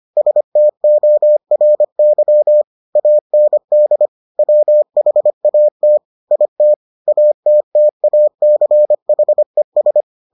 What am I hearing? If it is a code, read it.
Morse code, heard as a keyed tone, about 25 wpm, 600 Hz. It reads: STORY AND WHAT IT ATTACHES